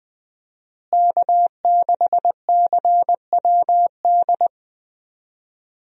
Morse code, heard as a keyed tone, about 20 wpm, 700 Hz. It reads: K6CWD
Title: USA Call Signs - 5 Characters - Repeated 2x - 20wpm